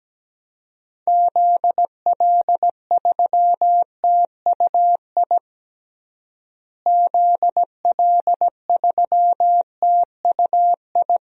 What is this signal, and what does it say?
Morse code, heard as a keyed tone, about 17 wpm, 700 Hz. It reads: ZL3TUI ZL3TUI
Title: DXCC-100 Call Signs - 6 Characters - Repeated 2x - 17wpm